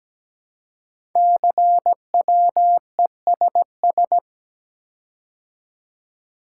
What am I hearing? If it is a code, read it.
Morse code, heard as a keyed tone, about 17 wpm, 700 Hz. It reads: CWESS
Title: Code Groups - 5 Letters - 17wpm